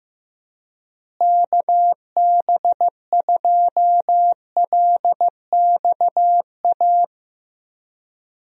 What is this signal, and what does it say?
Morse code, heard as a keyed tone, about 14 wpm, 700 Hz. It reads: KB2LXA
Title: USA Call Signs - 6 Characters - Repeated 3x - 15wpm